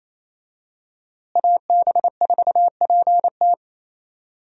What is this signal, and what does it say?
Morse code, heard as a keyed tone, about 28 wpm, 700 Hz. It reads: AB4PT
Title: USA Call Signs - 5 Characters - Repeated 3x - 28wpm